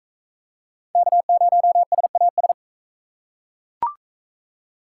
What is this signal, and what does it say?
Morse code, heard as a keyed tone, about 40 wpm, 700 Hz. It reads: K0SAS